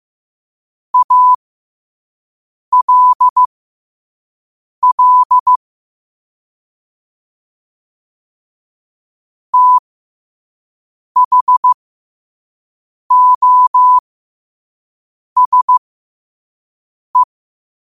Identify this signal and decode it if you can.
Morse code, heard as a keyed tone, about 15 wpm, spaced as for 5 wpm, 1000 Hz. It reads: ALL THOSE